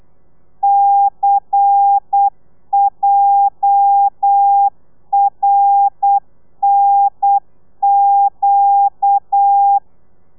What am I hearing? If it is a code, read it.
Morse code, heard as a keyed tone, about 8 wpm, 800 Hz. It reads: CJRNQ
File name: telegrafia / i2tfj / Lez11 35